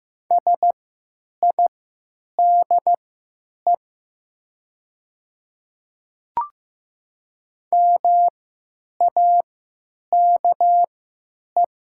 Morse code, heard as a keyed tone, about 15 wpm, spaced as for 9 wpm, 700 Hz. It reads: SIDE MAKE
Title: 3x Character Spacing - Top 200 Words - 15wpm